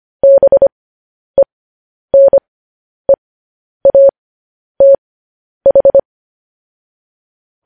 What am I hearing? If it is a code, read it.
Morse code, heard as a keyed tone, about 25 wpm, spaced as for 10 wpm, 550 Hz. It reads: BENEATH